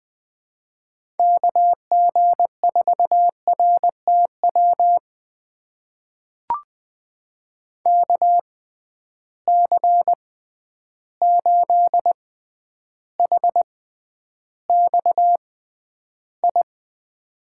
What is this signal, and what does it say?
Morse code, heard as a keyed tone, about 20 wpm, spaced as for 10 wpm, 700 Hz. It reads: KG4RTW KC8HXI
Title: USA Call Signs - 6 Characters - 6x Character Spacing - 20wpm